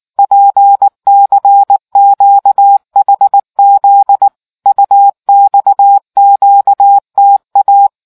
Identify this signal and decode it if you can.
Morse code, heard as a keyed tone, about 19 wpm, 800 Hz. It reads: PCQHZ UXQTA